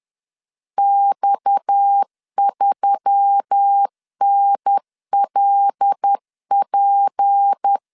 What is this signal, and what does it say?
Morse code, heard as a keyed tone, about 10 wpm, 800 Hz. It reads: X3NLP